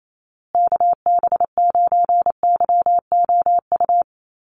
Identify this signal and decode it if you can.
Morse code, heard as a keyed tone, about 28 wpm, 700 Hz. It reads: KB9YOU
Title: USA Call Signs Repeated 2x 28wpm